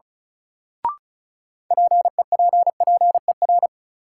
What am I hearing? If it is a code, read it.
Morse code, heard as a keyed tone, about 36 wpm, 700 Hz. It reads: PEPPER
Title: Sets of 5 Words - New Word Drill - 35wpm